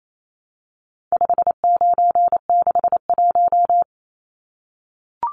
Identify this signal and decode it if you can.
Morse code, heard as a keyed tone, about 28 wpm, 700 Hz. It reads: 5961
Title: Numbers - 4-Digit 28wpm